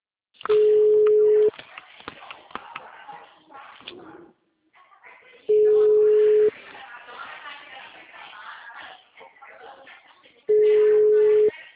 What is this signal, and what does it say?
Ringback tone (Europe, 425 Hz), with speech.
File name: out- -2010- - - .84722